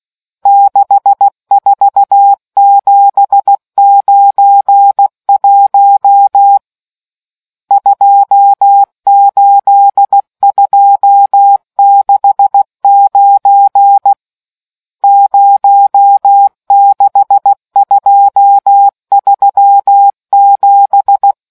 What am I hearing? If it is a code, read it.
Morse code, heard as a keyed tone, about 16 wpm, 800 Hz. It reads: 64791 28269 06237